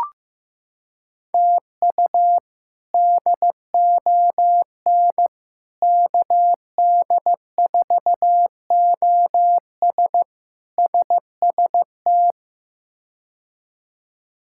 Morse code, heard as a keyed tone, about 15 wpm, 700 Hz. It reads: TU DON KD4OS SST
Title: K1USN SST QSO Practice - 15wpm